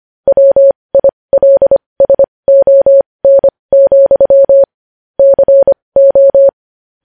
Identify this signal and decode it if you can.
Morse code, heard as a keyed tone, about 25 wpm, spaced as for 20 wpm, 550 Hz. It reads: WILSON, CO
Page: News Headlines in Morse Code at 20 WPM